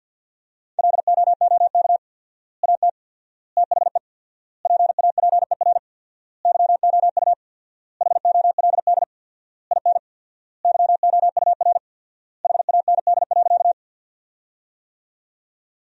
Morse code, heard as a keyed tone, about 50 wpm, 700 Hz. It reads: LOOK AT THE PAPER YOU HOLD IN YOUR HAND.
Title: 4x Word-Spacing - Sentences from Top 400 Words - 50wpm